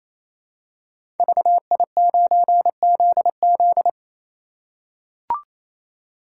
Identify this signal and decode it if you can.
Morse code, heard as a keyed tone, about 27 wpm, 700 Hz. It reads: VI9ZZ
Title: Code Groups - 5 Letter:Numbers - 28wpm